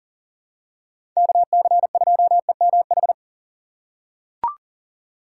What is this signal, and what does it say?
Morse code, heard as a keyed tone, about 40 wpm, 700 Hz. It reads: KC2EMH
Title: USA Call Signs - 6 Characters - 2x Character Spacing - 40wpm